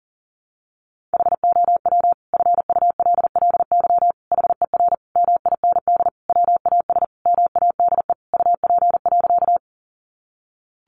Morse code, heard as a keyed tone, about 40 wpm, 700 Hz. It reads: HOW FULLY HER MIND WAS MADE UP.
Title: Sentences from Top 1300 Words 40wpm